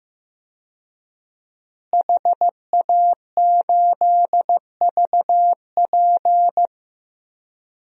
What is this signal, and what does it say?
Morse code, heard as a keyed tone, about 15 wpm, 700 Hz. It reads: HA8VP
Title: DXCC-100 Call Signs - 5 Characters - Repeated 3x - 15wpm